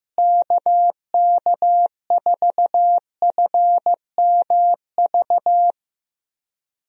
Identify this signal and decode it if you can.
Morse code, heard as a keyed tone, about 15 wpm, 700 Hz. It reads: KK4FMV
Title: USA Call Signs - 6 Characters - Repeated 2x - 15wpm